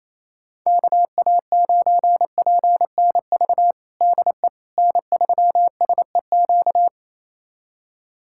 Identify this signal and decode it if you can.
Morse code, heard as a keyed tone, about 28 wpm, 700 Hz. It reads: KA9PNV DE N3SEQ
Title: USA Callsign DE Callsign 28wpm